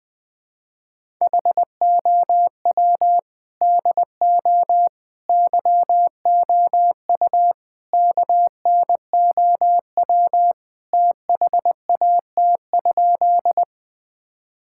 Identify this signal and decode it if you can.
Morse code, heard as a keyed tone, about 20 wpm, 700 Hz. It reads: HOW DO YOU KNOW THAT?
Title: Sentences with 4-Word Phrases - 20wpm